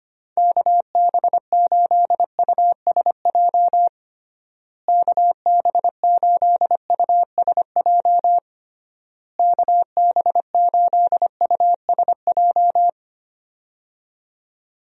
Morse code, heard as a keyed tone, about 25 wpm, 700 Hz. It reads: KB8USJ KB8USJ KB8USJ
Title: USA Call Signs - 6 Characters - Repeated 3x - 25wpm